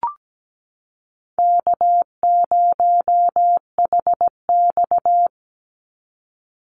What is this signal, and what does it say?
Morse code, heard as a keyed tone, about 17 wpm, 700 Hz. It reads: K0HX
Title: Call Signs Repeated 2x - 17wpm